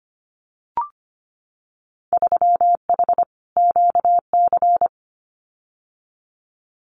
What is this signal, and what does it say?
Morse code, heard as a keyed tone, about 25 wpm, 700 Hz. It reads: 3H QC
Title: Winter Field Day Exchanges 25wpm